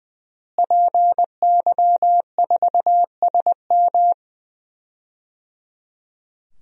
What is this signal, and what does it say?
Morse code, heard as a keyed tone, about 20 wpm, 700 Hz. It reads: PY4SM
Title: DXCC-100 Call Signs - 5 Characters - Repeated 3x - 20wpm